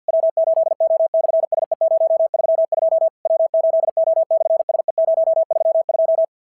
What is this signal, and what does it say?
Morse code, heard as a keyed tone, about 50 wpm, 650 Hz. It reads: W8O/SE032 W8O/SE032
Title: SOTA QSO Practice 50wpm